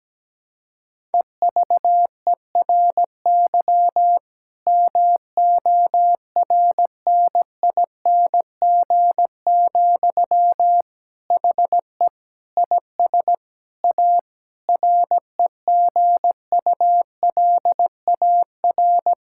Morse code, heard as a keyed tone, about 17 wpm, 700 Hz. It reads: EVERY MORNING, HE IS A REGULAR
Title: Sentences from Top 1000 Words - Review Part 6 - 17wpm